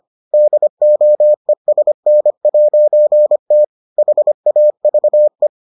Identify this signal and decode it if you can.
Morse code, heard as a keyed tone, about 25 wpm, 600 Hz. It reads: DOESN'T HAVE